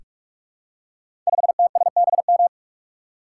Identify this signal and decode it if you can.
Morse code, heard as a keyed tone, about 44 wpm, 690 Hz. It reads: 5TSBM
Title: Code Groups - 5 Letter:Numbers - 45wpm